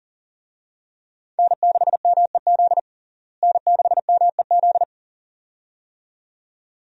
Morse code, heard as a keyed tone, about 40 wpm, 700 Hz. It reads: N6MEZ N6MEZ
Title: USA Call Signs - 5 Characters - Repeated 2x - 40wpm